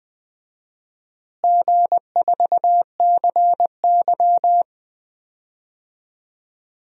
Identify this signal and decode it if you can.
Morse code, heard as a keyed tone, about 20 wpm, 700 Hz. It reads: G4CY
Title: DXCC-100 Call Signs - 4 Characters - Repeated 2x - 20wpm